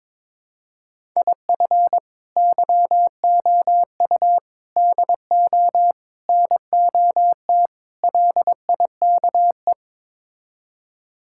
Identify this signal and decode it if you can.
Morse code, heard as a keyed tone, about 22 wpm, 700 Hz. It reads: IF YOU DO NOT LIKE
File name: practice / 22wpm / Build A Phrase - 5-Word Phrases - 22wpm